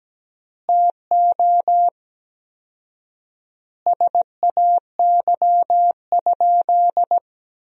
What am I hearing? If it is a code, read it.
Morse code, heard as a keyed tone, about 17 wpm, 700 Hz. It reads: TO SAY?
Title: 4x Word-Spacing - Sentences from Top 200 Words - 17wpm